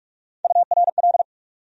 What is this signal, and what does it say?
Morse code, heard as a keyed tone, about 46 wpm, 700 Hz. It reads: URL
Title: USA Call Sign Postfixes - 3 Letters 45wpm